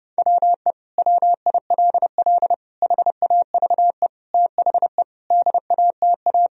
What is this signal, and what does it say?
Morse code, heard as a keyed tone, about 32 wpm, 700 Hz. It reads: WE WILL HAVE THE DATA
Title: Sentences from Top 1000 Words - Review Part 3 - 30wpm